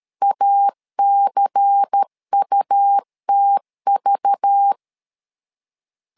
Morse code, heard as a keyed tone, about 13 wpm, 800 Hz. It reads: ACUTV
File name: LEZIONI / 5 LBVC / 5 LBVC 12wpm 1955